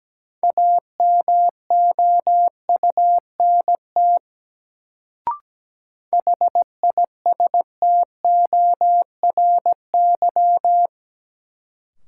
Morse code, heard as a keyed tone, about 17 wpm, 700 Hz. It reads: AMOUNT HISTORY
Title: Sets of 3 Words - New Word Drill - 17wpm